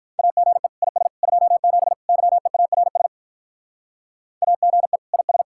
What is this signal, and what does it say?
Morse code, heard as a keyed tone, about 46 wpm, 700 Hz. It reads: AGE IS 27 YEARS AGE IS